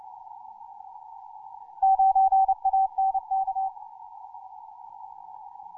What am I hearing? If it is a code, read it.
Morse code, heard as a keyed tone, about 29 wpm, 770 Hz. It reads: 9ANK
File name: simple-file-list / hk0na 10mcw